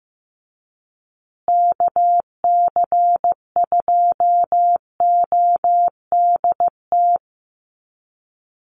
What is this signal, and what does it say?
Morse code, heard as a keyed tone, about 15 wpm, 700 Hz. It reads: KC2ODT
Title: USA Call Signs Repeated 2x 15wpm